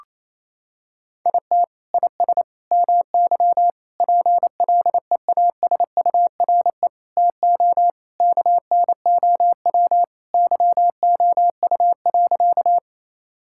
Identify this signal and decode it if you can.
Morse code, heard as a keyed tone, about 28 wpm, 700 Hz. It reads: IT IS MY PLEASURE TO KNOW YOU.
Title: Sentences from Top 1600 Words 28wpm